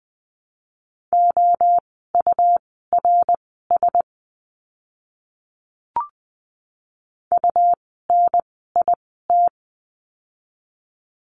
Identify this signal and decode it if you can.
Morse code, heard as a keyed tone, about 20 wpm, spaced as for 14 wpm, 700 Hz. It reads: OURS UNIT
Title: 101 Lesson 17f - Letter u - Words 20wpm